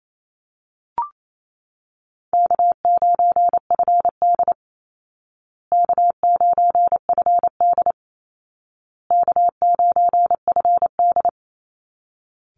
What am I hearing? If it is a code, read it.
Morse code, heard as a keyed tone, about 28 wpm, 700 Hz. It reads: K9FD K9FD K9FD
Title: Easy Call Signs Repeated 3x - 28wpm